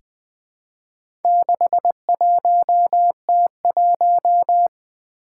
Morse code, heard as a keyed tone, about 21 wpm, 700 Hz. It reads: 61T1
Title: Four Letter-Number Combinations - 20wpm